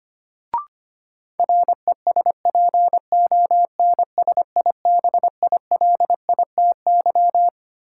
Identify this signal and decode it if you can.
Morse code, heard as a keyed tone, about 25 wpm, 700 Hz. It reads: RESPONSIBILITY
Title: Sentences from Top 1200 Words - New Word Drill - 25wpm